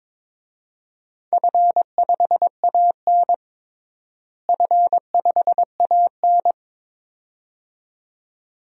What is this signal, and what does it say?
Morse code, heard as a keyed tone, about 22 wpm, 700 Hz. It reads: F5AN F5AN
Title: DXCC-100 Call Signs - 4 Characters - Repeated 2x - 22wpm